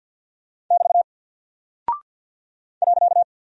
Morse code, heard as a keyed tone, about 48 wpm, 700 Hz. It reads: <BT> .
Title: Single Character (Letter:Number:Punctuation:Prosign) Alt 50wpm